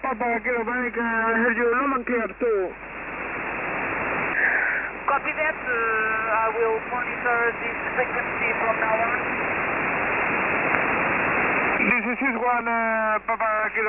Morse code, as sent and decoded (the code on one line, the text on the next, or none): none